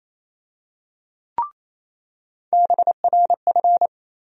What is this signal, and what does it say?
Morse code, heard as a keyed tone, about 27 wpm, 700 Hz. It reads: BRF
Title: USA Call Sign Postfixes - 3 Letters 28wpm